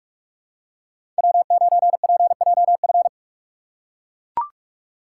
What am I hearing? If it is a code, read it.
Morse code, heard as a keyed tone, about 41 wpm, 700 Hz. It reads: W9PJF
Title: USA Call Signs - 5 Characters - 4x Character Spacing - 45wpm